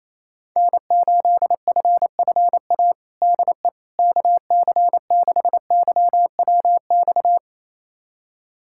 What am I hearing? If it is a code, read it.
Morse code, heard as a keyed tone, about 28 wpm, 700 Hz. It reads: N8FFA DE KC6YWX
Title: USA Callsign DE Callsign 28wpm